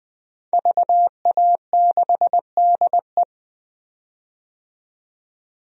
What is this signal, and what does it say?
Morse code, heard as a keyed tone, about 20 wpm, 700 Hz. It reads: VA6DE
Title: DXCC-100 Call Signs - 5 Characters - 20wpm